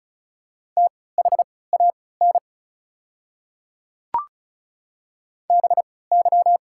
Morse code, heard as a keyed tone, about 36 wpm, spaced as for 20 wpm, 700 Hz. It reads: THAN BY
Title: 3x Character Spacing - Top 200 Words - 35wpm